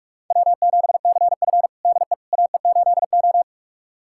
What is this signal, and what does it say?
Morse code, heard as a keyed tone, about 46 wpm, 700 Hz. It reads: W7CF DE AE8O
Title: USA Callsign DE Callsign - 4 Characters - 45wpm